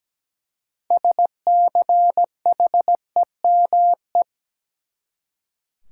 Morse code, heard as a keyed tone, about 17 wpm, 700 Hz. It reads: SCHEME
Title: Six Letter Words - 17wpm